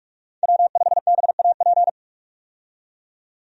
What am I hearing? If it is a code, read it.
Morse code, heard as a keyed tone, about 46 wpm, 700 Hz. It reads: W5BAP
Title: USA Call Signs - 5 Characters - 45wpm